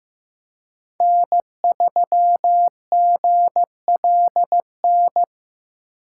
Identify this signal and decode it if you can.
Morse code, heard as a keyed tone, about 18 wpm, 700 Hz. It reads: N3GLN
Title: Speed Racing - USA 1x3 Call Signs 15wpm